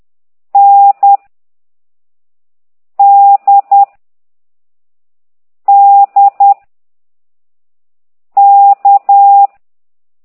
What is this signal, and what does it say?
Morse code, heard as a keyed tone, about 10 wpm, 800 Hz. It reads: N D D K